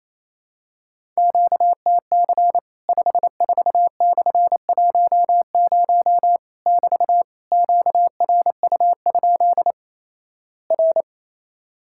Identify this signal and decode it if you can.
Morse code, heard as a keyed tone, about 28 wpm, 700 Hz. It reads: QTC 54/10 <BT> QRU? R
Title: Worked All Europe DX Contest - QTC Exchange - 28wpm